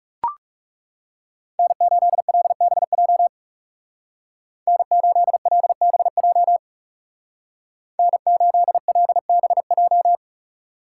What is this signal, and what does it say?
Morse code, heard as a keyed tone, about 37 wpm, 700 Hz. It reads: N8LBJ N8LBJ N8LBJ
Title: Speed Racing - USA 1x3 Call Signs 30wpm